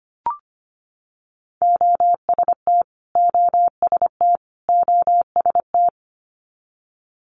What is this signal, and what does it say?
Morse code, heard as a keyed tone, about 25 wpm, 700 Hz. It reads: OST OST OST
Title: Multi-letter Phonemes Repeated 3x 25wpm